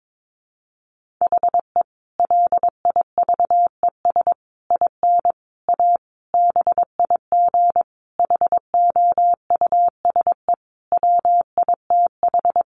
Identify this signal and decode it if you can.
Morse code, heard as a keyed tone, about 22 wpm, 700 Hz. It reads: HE LIVES IN A BIG HOUSE WITH
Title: Additional Easy Phrases 22wpm